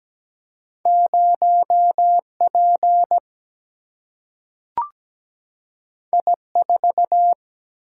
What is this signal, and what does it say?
Morse code, heard as a keyed tone, about 17 wpm, 700 Hz. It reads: 0P I4
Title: Single-Letter Single-Number 17wpm